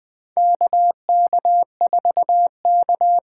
Morse code, heard as a keyed tone, about 21 wpm, 700 Hz. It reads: KK4K
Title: USA Call Signs - 4 Characters - 2x Character Spacing - 20wpm